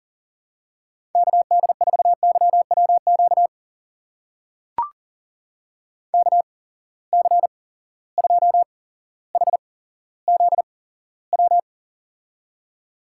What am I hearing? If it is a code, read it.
Morse code, heard as a keyed tone, about 38 wpm, spaced as for 17 wpm, 700 Hz. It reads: KD4YWQ KC2HZW
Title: USA Call Signs - 6 Characters - 8x Character Spacing - 40wpm